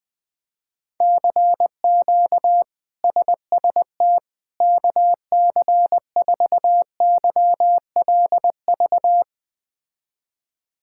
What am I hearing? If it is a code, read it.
Morse code, heard as a keyed tone, about 21 wpm, 700 Hz. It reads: CQ SST KC4YLV
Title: K1USN SST QSO Practice - 20wpm